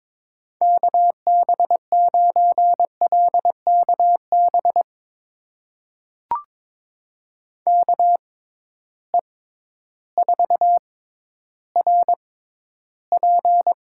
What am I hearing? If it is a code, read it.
Morse code, heard as a keyed tone, about 22 wpm, spaced as for 12 wpm, 700 Hz. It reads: KB9LKB KE4RP
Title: USA Call Signs - 6 Characters - 6x Character Spacing - 22wpm